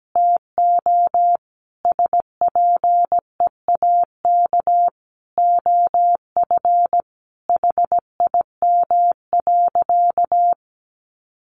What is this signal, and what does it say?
Morse code, heard as a keyed tone, about 17 wpm, 700 Hz. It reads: TO SPEAK OF HIM.
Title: Sentences from Top 1100 Words 17wpm